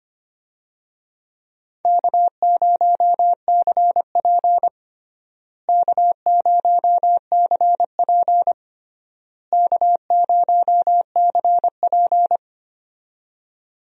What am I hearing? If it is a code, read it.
Morse code, heard as a keyed tone, about 25 wpm, 700 Hz. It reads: K0CP K0CP K0CP
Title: USA Call Signs - 4 Characters - Repeated 3x - 25wpm